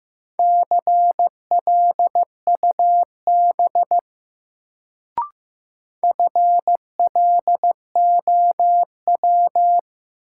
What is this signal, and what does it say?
Morse code, heard as a keyed tone, about 15 wpm, 700 Hz. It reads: CLUB FLOW
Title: Sentences from Top 1200 Words - New Word Drill - 15wpm